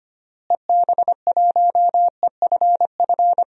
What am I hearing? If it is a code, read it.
Morse code, heard as a keyed tone, about 25 wpm, 700 Hz. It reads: EB1EFF
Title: DXCC-100 Call Signs - 6 Characters - Repeated 3x - 25wpm